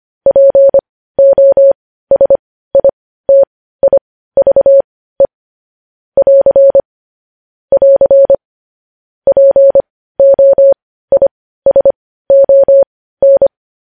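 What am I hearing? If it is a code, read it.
Morse code, heard as a keyed tone, about 25 wpm, spaced as for 15 wpm, 550 Hz. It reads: POSITIVE <AR> <AR> POISON
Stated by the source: Morse code sending of news headlines at 15 wpm.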